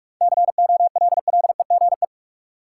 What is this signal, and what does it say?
Morse code, heard as a keyed tone, about 46 wpm, 700 Hz. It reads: COLLEGE
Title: Sentences from Top 1300 Words - New Word Drill Alt - 45wpm